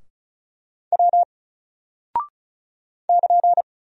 Morse code, heard as a keyed tone, about 36 wpm, 700 Hz. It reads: W <KN>
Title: Single Character (Letter:Number:Punctuation:Prosign) 35wpm